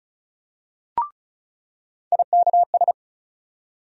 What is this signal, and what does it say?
Morse code, heard as a keyed tone, about 36 wpm, 700 Hz. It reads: IKS
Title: USA Call Sign Postfixes - 3 Letters 35wpm